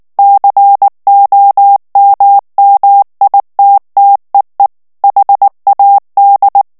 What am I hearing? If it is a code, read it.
Morse code, heard as a keyed tone, about 19 wpm, 800 Hz. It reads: COMMITTEE HAD
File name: ANGLAIS texte clair 3 vitesse 18 mots minute